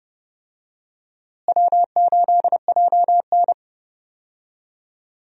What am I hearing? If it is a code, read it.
Morse code, heard as a keyed tone, about 30 wpm, 700 Hz. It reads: W8JN
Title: USA Call Signs - 4 Characters - Repeated 2x - 30wpm